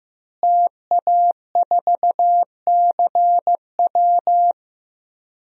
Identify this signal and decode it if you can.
Morse code, heard as a keyed tone, about 15 wpm, 700 Hz. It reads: TA4CW
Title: DXCC-100 Call Signs - 5 Characters - Repeated 2x - 15wpm